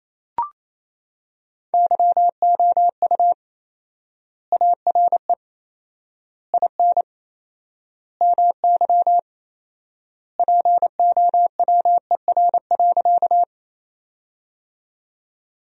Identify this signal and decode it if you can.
Morse code, heard as a keyed tone, about 27 wpm, 700 Hz. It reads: YOU ARE IN MY POWER.
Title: 4x Word-Spacing - Sentences from Top 500 Words - 28wpm